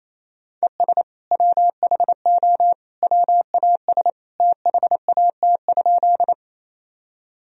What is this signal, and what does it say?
Morse code, heard as a keyed tone, about 28 wpm, 700 Hz. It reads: ES WHO WAS THAT?
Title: Sentences from Top 100 Words and Abbreviations 28wpm